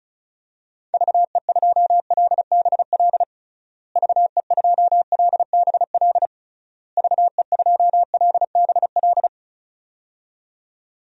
Morse code, heard as a keyed tone, about 34 wpm, 700 Hz. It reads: VE2LBL VE2LBL VE2LBL
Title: DXCC-100 Call Signs - 6 Characters - Repeated 3x - 35wpm